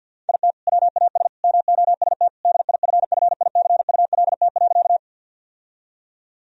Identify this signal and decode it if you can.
Morse code, heard as a keyed tone, about 52 wpm, 700 Hz. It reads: IT WAS MOST DIFFICULT.
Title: Sentences from Top 1000 Words - Review Part 1 - 50wpm